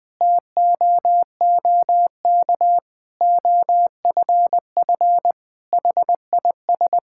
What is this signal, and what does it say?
Morse code, heard as a keyed tone, about 20 wpm, 700 Hz. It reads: TOOK OFF HIS
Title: Sentences from Top 1600 Words 20wpm